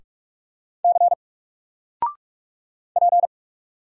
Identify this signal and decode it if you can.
Morse code, heard as a keyed tone, about 46 wpm, 700 Hz. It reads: C P
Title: Single Letter-Number 45wpm